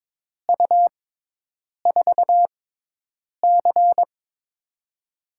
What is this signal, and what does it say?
Morse code, heard as a keyed tone, about 22 wpm, spaced as for 8 wpm, 700 Hz. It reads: U4C
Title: USA Call Signs - 4 Characters - 6x Character Spacing - 22wpm